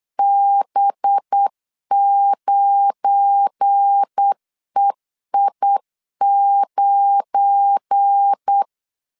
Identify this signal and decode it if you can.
Morse code, heard as a keyed tone, about 8 wpm, 800 Hz. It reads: B9EI9
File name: LEZIONI / 7 GF9Z / 7 GF9 8wpm 3545